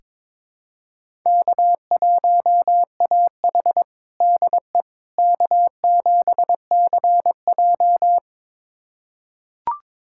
Morse code, heard as a keyed tone, about 22 wpm, 700 Hz. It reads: K1AH DE K7CJ
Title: USA Callsign DE Callsign - 4 Characters - 22wpm